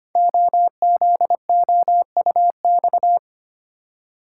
Morse code, heard as a keyed tone, about 25 wpm, 700 Hz. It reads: OZOUX
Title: Code Groups - 5 Letter:Numbers - 25wpm